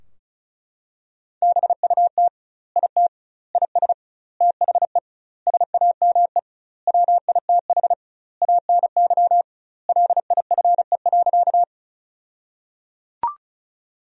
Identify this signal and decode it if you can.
Morse code, heard as a keyed tone, about 34 wpm, 700 Hz. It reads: BUT IT IS THE SAME WITH ANY LIFE.
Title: 2x Word-Spacing - Sentences from Top 300 Words - 35wpm